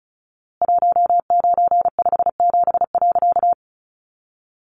Morse code, heard as a keyed tone, about 34 wpm, 700 Hz. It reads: 1957.
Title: A Sound Of Thunder 35wpm